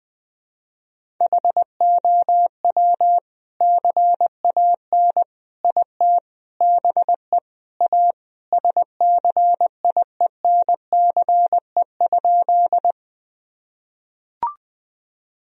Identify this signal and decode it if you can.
Morse code, heard as a keyed tone, about 20 wpm, 700 Hz. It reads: HOW CAN IT BE A SCIENCE?
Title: Sentences from Top 1000 Words - Review Part 2 - 20wpm